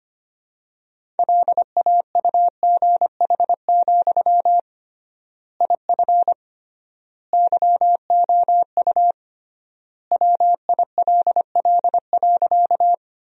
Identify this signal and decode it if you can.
Morse code, heard as a keyed tone, about 25 wpm, 700 Hz. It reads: LAUGH, IF YOU WILL.
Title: 3x Word-Spacing - Sentences from Top 500 Words - 25wpm